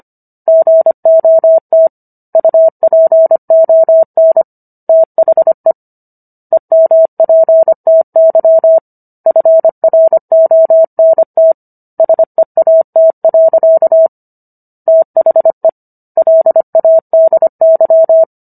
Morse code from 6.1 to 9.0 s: EMPTY

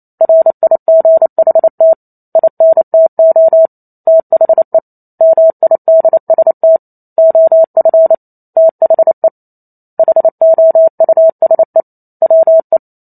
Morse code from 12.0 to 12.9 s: WE

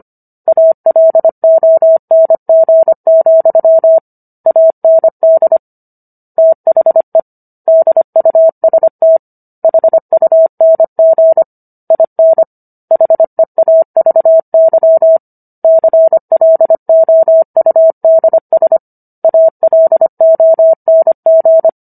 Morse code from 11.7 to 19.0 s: IN HEAVY CLOUDS

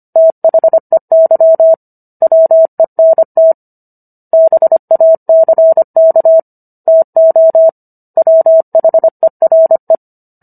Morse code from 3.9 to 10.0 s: BACK TO WHERE

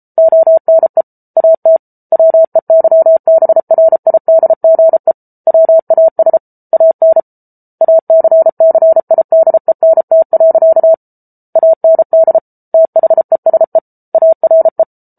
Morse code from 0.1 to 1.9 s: ONE AT

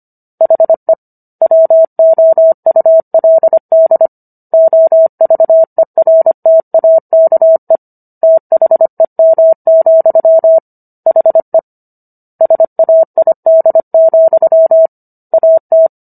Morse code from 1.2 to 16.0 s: WOULD OVERTAKE THEM, HE SAID, AT

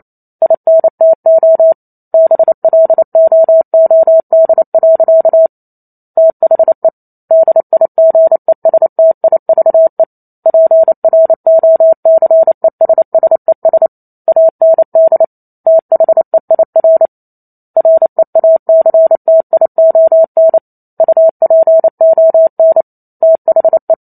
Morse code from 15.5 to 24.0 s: THEIR REACTION UPON THE

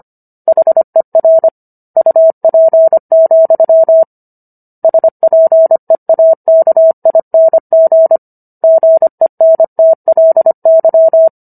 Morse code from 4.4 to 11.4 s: SPEAKING GENTLY

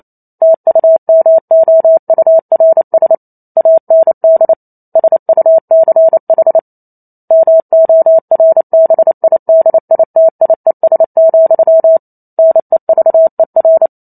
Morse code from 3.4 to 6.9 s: AND SUCH

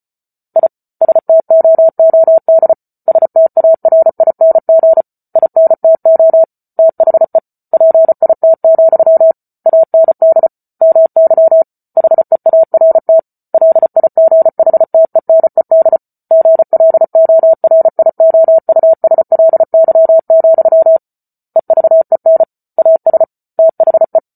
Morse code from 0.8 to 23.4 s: STOOD STARING INTO THE PIT, AND MY HEART LIGHTENED GLORIOUSLY, EVEN AS

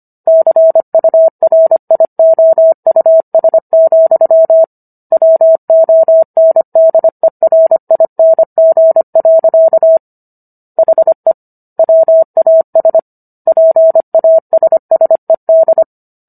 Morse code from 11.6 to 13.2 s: WAS

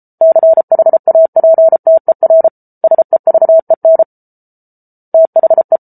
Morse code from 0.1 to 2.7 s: CHAPTER

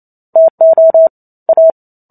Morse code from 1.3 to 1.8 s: A

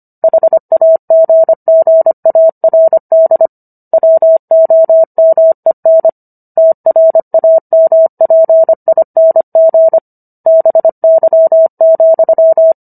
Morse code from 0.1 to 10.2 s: HAGGARD WOMEN TRAMPING

